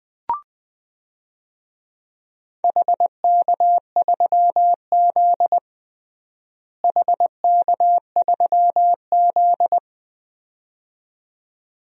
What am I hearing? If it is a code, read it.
Morse code, heard as a keyed tone, about 21 wpm, 700 Hz. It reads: HK3Z HK3Z
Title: DXCC-100 Call Signs - 4 Characters - Repeated 2x - 20wpm